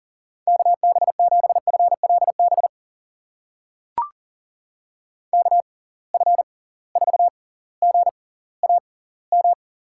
Morse code, heard as a keyed tone, about 40 wpm, spaced as for 20 wpm, 700 Hz. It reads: KB7FLB KF4GAM
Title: USA Call Signs - 6 Characters - 6x Character Spacing - 40wpm